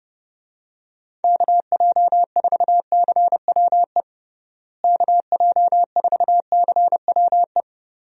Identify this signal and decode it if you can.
Morse code, heard as a keyed tone, about 32 wpm, 700 Hz. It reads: KJ4CWE KJ4CWE
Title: USA Call Signs - 6 Characters - Repeated 2x - 30wpm